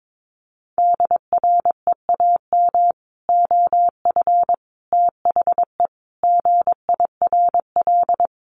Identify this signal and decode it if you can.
Morse code, heard as a keyed tone, about 22 wpm, 700 Hz. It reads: DREAM OF THE GIRL
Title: Sentences from Top 1500 Words 22wpm